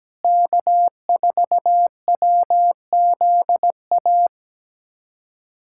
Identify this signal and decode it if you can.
Morse code, heard as a keyed tone, about 17 wpm, 700 Hz. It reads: K4WZA
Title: USA Callsign DE Callsign - 5 Characters - 17wpm